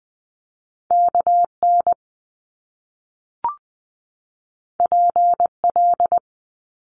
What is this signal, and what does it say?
Morse code, heard as a keyed tone, about 20 wpm, 700 Hz. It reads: KN PL
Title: Top Two Letter Combinations in Words 20wpm